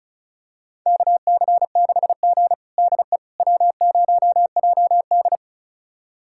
Morse code, heard as a keyed tone, about 36 wpm, 700 Hz. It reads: KC6G DE W0JD
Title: USA Callsign DE Callsign - 4 Characters - 35wpm